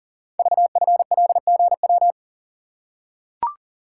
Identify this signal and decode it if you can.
Morse code, heard as a keyed tone, about 39 wpm, 700 Hz. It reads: VFLGW
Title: Code Groups - 5 Letters - 40wpm